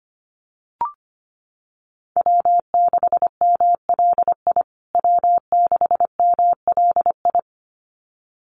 Morse code, heard as a keyed tone, about 25 wpm, 700 Hz. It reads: W6MLI W6MLI
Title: USA Call Signs Repeated 2x 25wpm